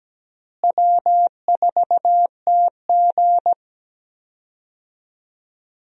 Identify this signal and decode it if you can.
Morse code, heard as a keyed tone, about 17 wpm, 700 Hz. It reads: W4TG
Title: USA Call Signs - 4 Characters - Repeated 3x - 17wpm